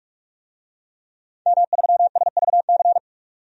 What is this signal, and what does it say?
Morse code, heard as a keyed tone, about 46 wpm, 700 Hz. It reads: M3SVC
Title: DXCC-100 Call Signs - 5 Characters - 45wpm